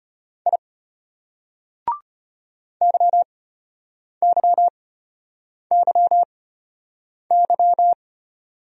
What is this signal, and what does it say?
Morse code, heard as a keyed tone, about 32 wpm, spaced as for 8 wpm, 700 Hz. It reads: I YYYY